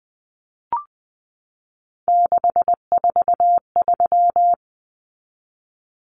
Morse code, heard as a keyed tone, about 20 wpm, 700 Hz. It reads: 643
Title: Common Cut Numbers - 3-Digit 20wpm